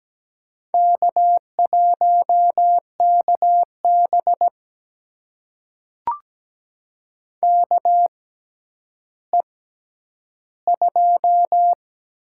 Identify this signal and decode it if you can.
Morse code, heard as a keyed tone, about 17 wpm, spaced as for 10 wpm, 700 Hz. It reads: K1KB KE2
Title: USA Call Signs - 4 Characters - 6x Character Spacing - 17wpm